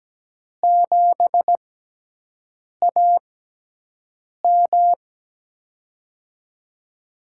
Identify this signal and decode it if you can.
Morse code, heard as a keyed tone, about 17 wpm, spaced as for 6 wpm, 700 Hz. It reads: 7AM
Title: USA Call Signs - 4 Characters - 6x Character Spacing - 17wpm